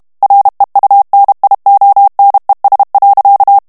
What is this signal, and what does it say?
Morse code, heard as a keyed tone, about 32 wpm, 800 Hz. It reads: REUNIONES.
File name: morse mp3 / ESPAGNOL TEXTE EN CLAIR 1 VITESSE 30 MOTS MINUTES